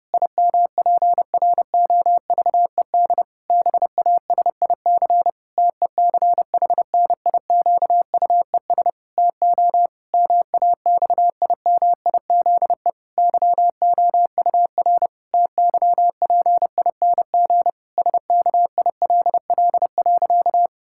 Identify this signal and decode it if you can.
Morse code, heard as a keyed tone, about 30 wpm, 700 Hz. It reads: IMPROVED BASIC TECHNIQUES TO MAXIMIZE YOUR TYPING SKILL.
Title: Pangram Challenge v2 - 30wpm